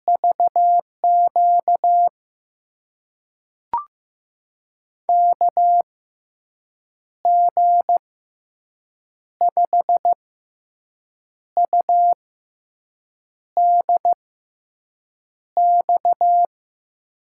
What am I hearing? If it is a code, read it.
Morse code, heard as a keyed tone, about 15 wpm, spaced as for 6 wpm, 700 Hz. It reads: VQ KG5UDX
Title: USA Call Signs - 6 Characters - 6x Character Spacing - 15wpm